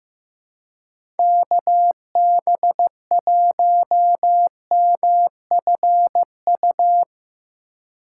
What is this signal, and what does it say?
Morse code, heard as a keyed tone, about 15 wpm, 700 Hz. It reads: KB1MFU
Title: USA Call Signs - 6 Characters - 2x Character Spacing - 15wpm